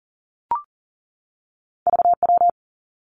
Morse code, heard as a keyed tone, about 40 wpm, 700 Hz. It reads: VW
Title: Two Challenging Letter-Number Combinations 40wpm